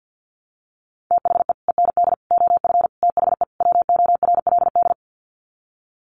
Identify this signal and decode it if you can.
Morse code, heard as a keyed tone, about 50 wpm, 700 Hz. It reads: THE END OF THE WORLD
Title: Sets of 5 Words 50wpm